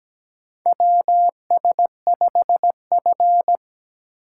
Morse code, heard as a keyed tone, about 17 wpm, 700 Hz. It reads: WS5F
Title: USA Call Signs - 4 Characters - 8x Character Spacing - 17wpm